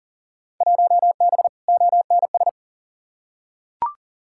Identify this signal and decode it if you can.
Morse code, heard as a keyed tone, about 40 wpm, 700 Hz. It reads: 1B ONS
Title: Field Day Exchanges Encore - 40wpm